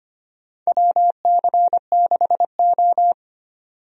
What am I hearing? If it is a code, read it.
Morse code, heard as a keyed tone, about 25 wpm, 700 Hz. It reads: WC6O
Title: USA Call Signs - 4 Characters - 25wpm